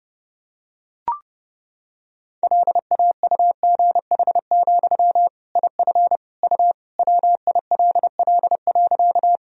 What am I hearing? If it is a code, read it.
Morse code, heard as a keyed tone, about 30 wpm, 700 Hz. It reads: LAUGH, IF U WILL.
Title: Sentences from Top 500 Words and Abbreviations 30wpm